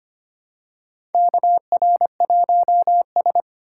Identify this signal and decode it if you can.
Morse code, heard as a keyed tone, about 25 wpm, 700 Hz. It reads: KR1S
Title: USA Call Signs - 4 Characters - 6x Character Spacing - 25wpm